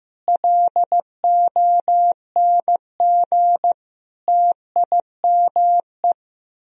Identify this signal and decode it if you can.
Morse code, heard as a keyed tone, about 15 wpm, 700 Hz. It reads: LONG TIME
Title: Speed-Racing - Sets of 5 Words 15wpm